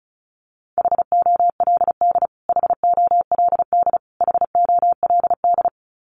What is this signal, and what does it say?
Morse code, heard as a keyed tone, about 34 wpm, 700 Hz. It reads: HOLD HOLD HOLD
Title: Top 500 Words Repeated 35wpm